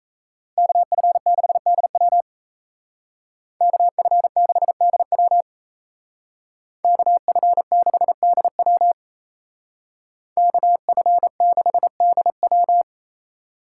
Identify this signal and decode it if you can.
Morse code, heard as a keyed tone, about 36 wpm, 700 Hz. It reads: KF6DW KF6DW KF6DW KF6DW